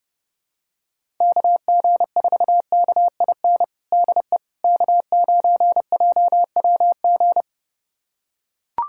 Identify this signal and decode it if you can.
Morse code, heard as a keyed tone, about 30 wpm, 700 Hz. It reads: KG4KIN DE K9JWG
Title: USA Callsign DE Callsign 30wpm